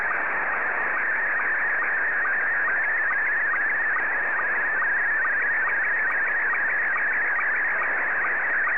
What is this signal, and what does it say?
SSTV (slow-scan television), Scottie 1.